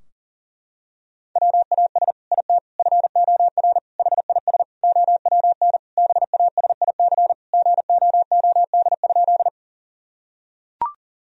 Morse code, heard as a keyed tone, about 40 wpm, 700 Hz. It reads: WAS IT FOR HIS OWN BASIC GOOD?
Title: Sentences from Top 1000 Words - Review Part 3 - 40wpm